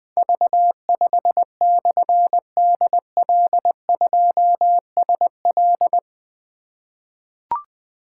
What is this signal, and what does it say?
Morse code, heard as a keyed tone, about 20 wpm, 700 Hz. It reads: V5/DL2SL
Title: Difficult Call Signs 20wpm